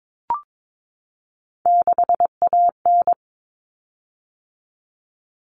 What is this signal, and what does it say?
Morse code, heard as a keyed tone, about 22 wpm, 700 Hz. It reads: 6AN
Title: Common Cut Numbers - 3-Digit 22wpm